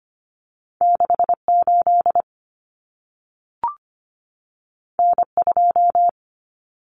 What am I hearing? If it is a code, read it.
Morse code, heard as a keyed tone, about 25 wpm, 700 Hz. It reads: 68 N2
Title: Common Cut Numbers - 2-Digit 25wpm